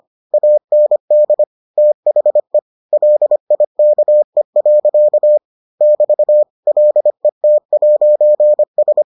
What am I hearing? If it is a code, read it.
Morse code, heard as a keyed tone, about 25 wpm, 600 Hz. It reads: AND THE LIKE. <BT> LET'S